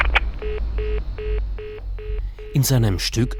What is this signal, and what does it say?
Congestion tone (425 Hz), with speech.